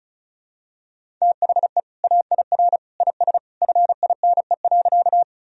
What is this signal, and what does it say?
Morse code, heard as a keyed tone, about 34 wpm, 700 Hz. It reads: THE AIR IS FINE.
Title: Sentences from Top 1000 Words - Review Part 1 - 35wpm